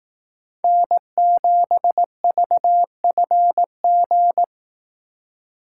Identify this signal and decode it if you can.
Morse code, heard as a keyed tone, about 18 wpm, 700 Hz. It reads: N7VFG
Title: Speed Racing - USA Call Signs 15wpm